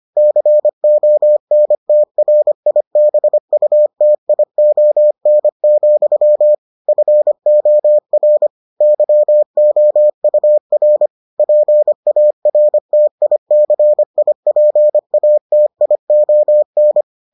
Morse code, heard as a keyed tone, about 25 wpm, 600 Hz. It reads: CONTRIBUTION, FOR YOUR PARTICIPATION